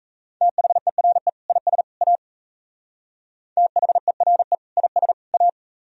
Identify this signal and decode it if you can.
Morse code, heard as a keyed tone, about 34 wpm, 700 Hz. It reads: THERE IS A THERE IS A
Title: Speed-Racing - Sets of 3 Words 28wpm